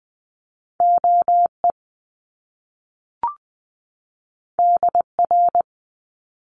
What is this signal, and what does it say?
Morse code, heard as a keyed tone, about 20 wpm, 700 Hz. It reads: OE DR
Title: Top Two Letter Combinations in Words 20wpm